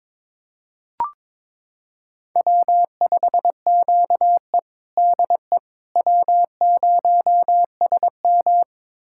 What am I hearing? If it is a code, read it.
Morse code, heard as a keyed tone, about 21 wpm, 700 Hz. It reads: W5QE DE W0SM
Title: USA Callsign DE Callsign - 4 Characters - 22wpm